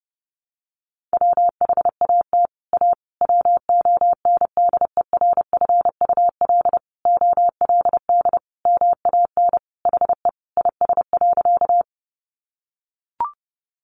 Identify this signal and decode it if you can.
Morse code, heard as a keyed tone, about 30 wpm, 700 Hz. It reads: WHAT A WONDERFUL OLD MAN HE IS.
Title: Sentences from Top 1100 Words 30wpm